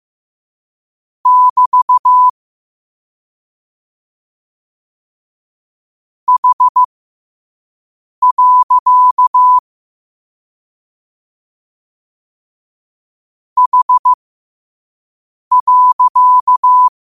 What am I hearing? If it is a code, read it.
Morse code, heard as a keyed tone, about 15 wpm, spaced as for 5 wpm, 1000 Hz. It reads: <BT> H. H.
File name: qotdfiles / 5wpm